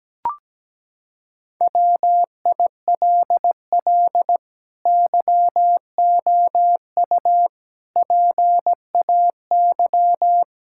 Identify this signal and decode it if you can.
Morse code, heard as a keyed tone, about 17 wpm, 700 Hz. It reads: WILL YOU PAY
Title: Sentences from Top 1600 Words 17wpm